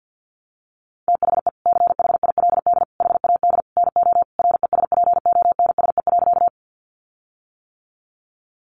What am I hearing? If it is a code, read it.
Morse code, heard as a keyed tone, about 50 wpm, 700 Hz. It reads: THE CHILD HAD NO RESPONSE.
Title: Sentences from Top 1400 Words 50wpm